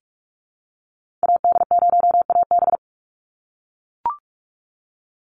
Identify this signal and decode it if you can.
Morse code, heard as a keyed tone, about 44 wpm, 700 Hz. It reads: AD0AB
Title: Call Signs - 45wpm